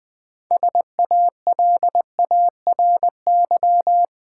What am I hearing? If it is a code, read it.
Morse code, heard as a keyed tone, about 20 wpm, 700 Hz. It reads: SALARY
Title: Six Letter Words - 20wpm